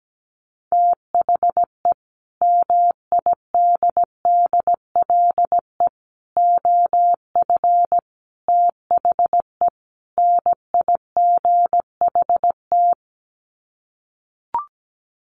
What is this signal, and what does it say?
Morse code, heard as a keyed tone, about 17 wpm, 700 Hz. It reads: THE MIDDLE OF THE NIGHT
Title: Sets of 5 Words 17wpm